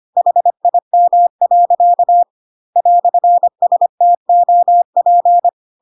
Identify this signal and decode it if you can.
Morse code, heard as a keyed tone, about 25 wpm, 700 Hz. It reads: HIM. "STOP
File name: Pinocchio / Pinocchio ch3 25 25 ik1hge